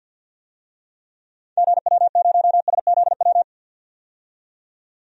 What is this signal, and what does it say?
Morse code, heard as a keyed tone, about 53 wpm, 700 Hz. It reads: GW0SZW
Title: DXCC-100 Call Signs - 6 Characters - 50wpm